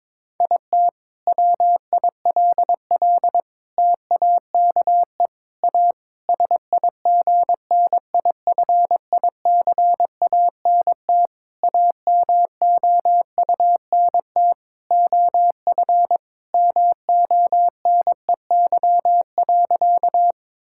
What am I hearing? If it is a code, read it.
Morse code, heard as a keyed tone, about 22 wpm, 700 Hz. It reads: IT WILL TAKE A SIGNIFICANT AMOUNT OF MONEY.
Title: Sentences from Top 1000 Words - Review Part 6 - 22wpm